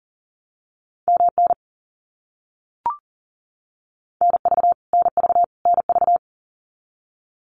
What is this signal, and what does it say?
Morse code, heard as a keyed tone, about 40 wpm, 700 Hz. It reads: MN NV NV NV
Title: US States Repeated 3x 40wpm